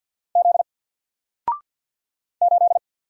Morse code, heard as a keyed tone, about 48 wpm, 700 Hz. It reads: Z 8
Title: Single Letter-Number Alt 50wpm